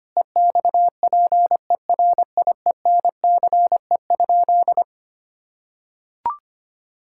Morse code, heard as a keyed tone, about 25 wpm, 700 Hz. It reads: EXPERIENCE?
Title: Sentences from Top 1600 Words 25wpm